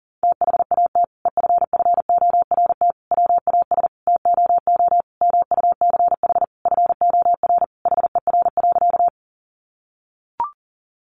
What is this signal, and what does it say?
Morse code, heard as a keyed tone, about 37 wpm, 700 Hz. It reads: THAT EFFORT WAS TOO MUCH FOR HER.
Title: Sentences from Top 1200 Words 40wpm